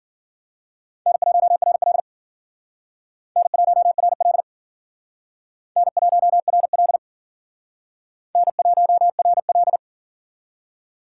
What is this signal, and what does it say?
Morse code, heard as a keyed tone, about 54 wpm, 700 Hz. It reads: N1RL N1RL N1RL N1RL